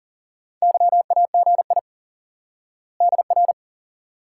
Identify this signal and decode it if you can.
Morse code, heard as a keyed tone, about 38 wpm, 700 Hz. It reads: YAGI DR
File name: IQE Course 3 - Lesson 89 - YAGI - Two Elements - 40wpm